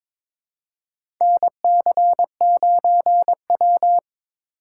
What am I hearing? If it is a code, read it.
Morse code, heard as a keyed tone, about 21 wpm, 700 Hz. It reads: NC9W
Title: K1USN SST QSO Practice - 22wpm